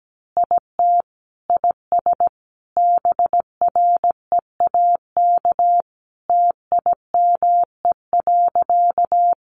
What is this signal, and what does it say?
Morse code, heard as a keyed tone, about 17 wpm, 700 Hz. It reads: IT IS BREAK TIME.
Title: Easy Phrases 17wpm